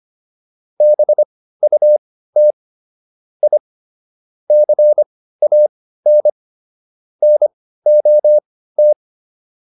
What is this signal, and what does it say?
Morse code, heard as a keyed tone, about 25 wpm, spaced as for 15 wpm, 600 Hz. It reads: BUT I CAN NOT